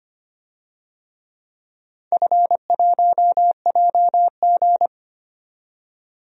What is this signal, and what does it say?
Morse code, heard as a keyed tone, about 25 wpm, 700 Hz. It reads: F1JG
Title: DXCC-100 Call Signs - 4 Characters - 25wpm